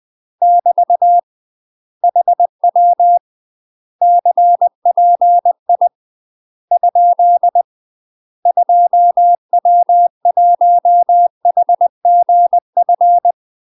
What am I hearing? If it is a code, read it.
Morse code, heard as a keyed tone, about 20 wpm, 700 Hz. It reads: <BT> HW CPI ? 2W1HGF
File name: UK MASTER QSOs TRAINING / 20WPM / UK MASTER QSOs 0004